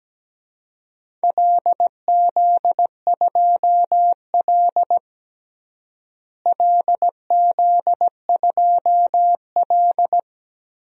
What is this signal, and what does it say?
Morse code, heard as a keyed tone, about 17 wpm, 700 Hz. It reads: LZ2L LZ2L
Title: DXCC-100 Call Signs - 4 Characters - Repeated 3x - 17wpm